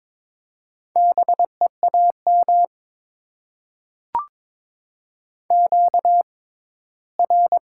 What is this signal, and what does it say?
Morse code, heard as a keyed tone, about 22 wpm, spaced as for 15 wpm, 700 Hz. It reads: BEAM QR
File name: Top 90 QSO Elements - 6x Character Spacing - 22wpm